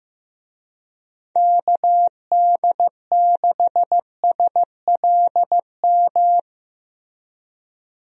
Morse code, heard as a keyed tone, about 15 wpm, 700 Hz. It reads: KD6SLM
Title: USA Call Signs - 6 Characters - Repeated 2x - 15wpm